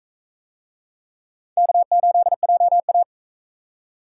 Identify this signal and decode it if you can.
Morse code, heard as a keyed tone, about 37 wpm, 700 Hz. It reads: K8JA